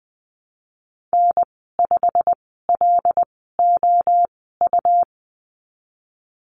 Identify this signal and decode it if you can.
Morse code, heard as a keyed tone, about 20 wpm, spaced as for 15 wpm, 700 Hz. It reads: N5LOU
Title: 179 Lesson 29g - Slash - Call Signs 20wpm